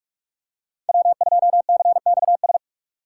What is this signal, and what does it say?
Morse code, heard as a keyed tone, about 44 wpm, 700 Hz. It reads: W2CXS
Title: Code Groups - 5 Letter:Numbers - 45wpm